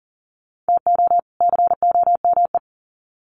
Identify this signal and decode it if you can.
Morse code, heard as a keyed tone, about 40 wpm, 700 Hz. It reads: TO COME
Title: Sets of 2 Words 40wpm